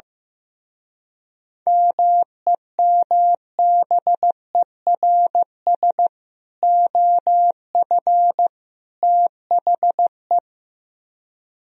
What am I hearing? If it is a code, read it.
Morse code, heard as a keyed tone, about 15 wpm, 700 Hz. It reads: MEMBERS OF THE
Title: Speed-Racing - Sets of 3 Words 15wpm